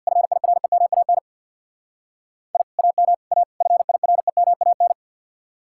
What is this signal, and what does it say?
Morse code, heard as a keyed tone, about 62 wpm, 700 Hz. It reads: FIREMAN I AM A FIREMAN
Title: Speed-Racing Common QSO Phrases Encore - 50wpm